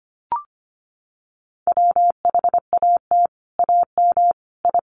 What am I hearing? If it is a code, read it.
Morse code, heard as a keyed tone, about 25 wpm, 700 Hz. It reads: WHAT AM I
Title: Sets of 5 Words Encore2 25wpm